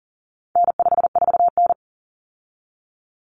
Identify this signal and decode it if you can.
Morse code, heard as a keyed tone, about 40 wpm, 700 Hz. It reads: N54N
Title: Common Cut Numbers - 4-Digit 40wpm